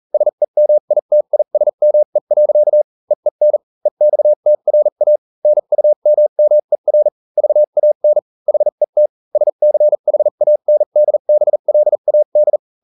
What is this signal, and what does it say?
Morse code, heard as a keyed tone, about 40 wpm, spaced as for 35 wpm, 600 Hz. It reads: SEMITISME. EEN EXTRA NUMMER VAN HET SCHANDBLAD